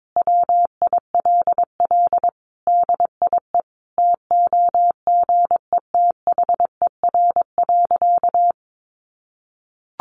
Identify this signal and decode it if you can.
Morse code, heard as a keyed tone, about 22 wpm, 700 Hz. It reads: WILL DIE TOGETHER.
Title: Sentences from Top 1300 Words 22wpm